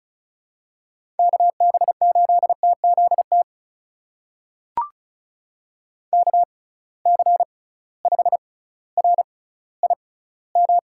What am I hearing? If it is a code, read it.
Morse code, heard as a keyed tone, about 36 wpm, spaced as for 18 wpm, 700 Hz. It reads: KB8TZT KC5RIM
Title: USA Call Signs - 6 Characters - 6x Character Spacing - 35wpm